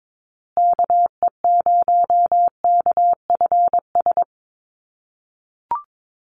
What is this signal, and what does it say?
Morse code, heard as a keyed tone, about 22 wpm, 700 Hz. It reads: KE0KFS
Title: USA Call Signs Repeated 2x 22wpm